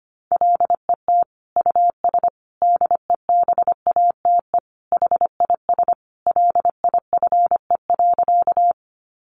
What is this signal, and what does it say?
Morse code, heard as a keyed tone, about 25 wpm, 700 Hz. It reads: LET US DEBATE HIS LIFE.
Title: Sentences from Top 1100 Words 25wpm